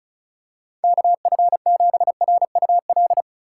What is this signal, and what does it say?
Morse code, heard as a keyed tone, about 36 wpm, 700 Hz. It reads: KF7RUL
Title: USA Call Signs - 6 Characters - 4x Character Spacing - 35wpm